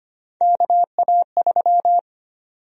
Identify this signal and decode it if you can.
Morse code, heard as a keyed tone, about 25 wpm, 700 Hz. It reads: KA3
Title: USA Call Sign Prefixes - Repeated 2x - 25wpm